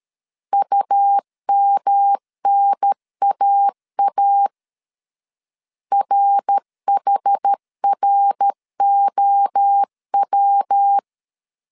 Morse code, heard as a keyed tone, about 13 wpm, 800 Hz. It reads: UMNAA RHROW
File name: CW / DET / 1 3 1500C S C 12wpm 2750